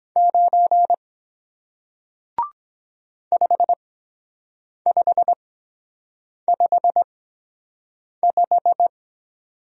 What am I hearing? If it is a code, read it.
Morse code, heard as a keyed tone, about 24 wpm, spaced as for 7 wpm, 700 Hz. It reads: 9 5555